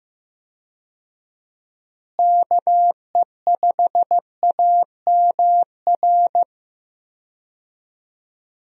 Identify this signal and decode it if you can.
Morse code, heard as a keyed tone, about 15 wpm, 700 Hz. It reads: KE5AMR
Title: USA Call Signs - 6 Characters - Repeated 3x - 15wpm